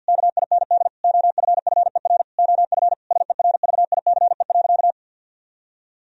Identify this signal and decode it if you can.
Morse code, heard as a keyed tone, about 48 wpm, 700 Hz. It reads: KIND OFFER OF SERVICE.
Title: Sentences from Top 1000 Words - Review Part 4 - 50wpm